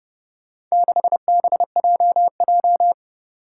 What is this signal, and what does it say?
Morse code, heard as a keyed tone, about 32 wpm, 700 Hz. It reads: 6BJJ
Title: Four Challenging Letter-Number Combinations 30wpm